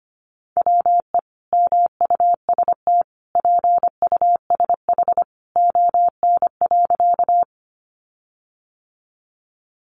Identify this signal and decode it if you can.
Morse code, heard as a keyed tone, about 25 wpm, 700 Hz. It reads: WE MUST PUSH ON.
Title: Sentences from Top 1300 Words 25wpm